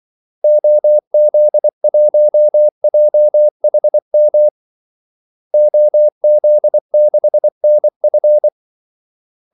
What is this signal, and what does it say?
Morse code, heard as a keyed tone, about 24 wpm, 600 Hz. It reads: OZ1JHM OZ6NF
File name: Callsigns-24wpm